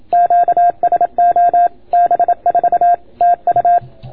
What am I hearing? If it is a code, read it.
Morse code, heard as a keyed tone, about 26 wpm, 690 Hz. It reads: QSO B4 TU